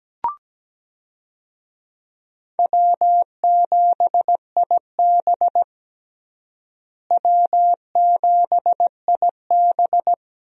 Morse code, heard as a keyed tone, about 17 wpm, 700 Hz. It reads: W7IB W7IB
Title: USA Call Signs - 4 Characters - Repeated 2x - 17wpm